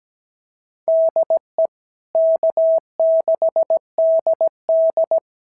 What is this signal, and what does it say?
Morse code, heard as a keyed tone, about 17 wpm, 650 Hz. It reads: DE K6DD
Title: SOTA QSO Practice 17wpm